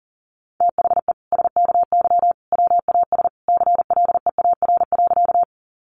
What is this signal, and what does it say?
Morse code, heard as a keyed tone, about 40 wpm, 700 Hz. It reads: THE SKY WAS CLEAR.
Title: Sentences from Top 1200 Words 40wpm